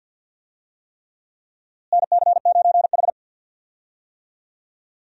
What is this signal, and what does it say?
Morse code, heard as a keyed tone, about 51 wpm, 700 Hz. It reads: NC9H
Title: USA Call Signs - 4 Characters - Repeated 3x - 50wpm